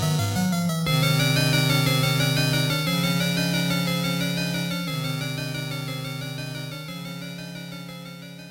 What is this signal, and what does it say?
There is no code.